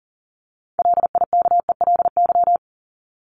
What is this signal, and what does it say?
Morse code, heard as a keyed tone, about 40 wpm, 700 Hz. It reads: LIKELY
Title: LY Suffix - Top 100 Words with Base Words 40wpm